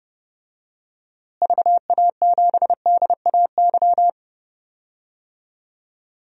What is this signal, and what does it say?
Morse code, heard as a keyed tone, about 30 wpm, 700 Hz. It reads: VA7DAY
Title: DXCC-100 Call Signs - 6 Characters - Repeated 3x - 30wpm